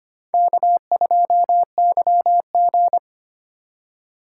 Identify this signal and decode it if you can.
Morse code, heard as a keyed tone, about 25 wpm, 700 Hz. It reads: K2YG
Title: USA Call Signs - 4 Characters - 3x Character Spacing - 25wpm